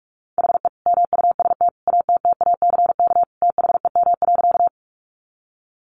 Morse code, heard as a keyed tone, about 44 wpm, 700 Hz. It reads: HE MUST ATTACK THEM.
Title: Sentences from Top 1500 Words 45wpm